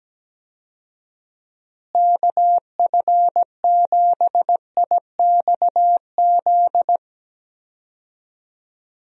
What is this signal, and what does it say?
Morse code, heard as a keyed tone, about 17 wpm, 700 Hz. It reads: KF7IXZ
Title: USA Call Signs - 6 Characters - Repeated 2x - 17wpm